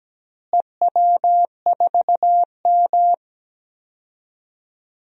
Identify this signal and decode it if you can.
Morse code, heard as a keyed tone, about 17 wpm, 700 Hz. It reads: EW4M
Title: DXCC-100 Call Signs - 4 Characters - Repeated 3x - 17wpm